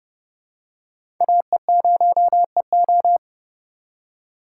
Morse code, heard as a keyed tone, about 32 wpm, 700 Hz. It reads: AE0EO
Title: USA Call Signs - 5 Characters - 3x Character Spacing - 30wpm